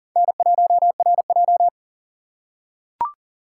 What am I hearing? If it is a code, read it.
Morse code, heard as a keyed tone, about 43 wpm, 690 Hz. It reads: N1RJ
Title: USA Call Signs - 4x Character Spacing - 40wpm